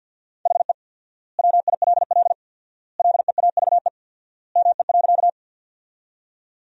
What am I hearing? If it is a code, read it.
Morse code, heard as a keyed tone, about 50 wpm, 700 Hz. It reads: HE WILL LEAVE ME.
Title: 4x Word-Spacing - Sentences from Top 400 Words - 50wpm